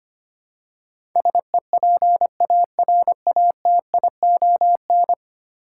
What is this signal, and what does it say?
Morse code, heard as a keyed tone, about 25 wpm, 700 Hz. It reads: SEPARATION
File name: Ten Letter Words - 25wpm